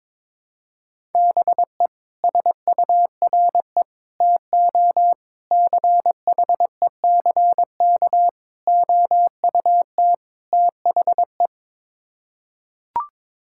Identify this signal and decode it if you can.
Morse code, heard as a keyed tone, about 22 wpm, 700 Hz. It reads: BE SURE TO CHECK OUT THE
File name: practice / 22wpm / Sets of 6 Words - 22wpm